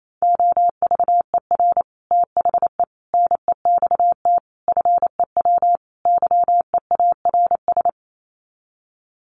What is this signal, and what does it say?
Morse code, heard as a keyed tone, about 28 wpm, 700 Hz. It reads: OVER THE NEXT FEW YEARS
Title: Sets of 5 Words Encore 28wpm